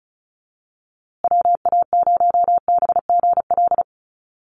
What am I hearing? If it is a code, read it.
Morse code, heard as a keyed tone, about 34 wpm, 700 Hz. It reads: WA0BGL
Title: USA Call Signs Repeated 3x 35wpm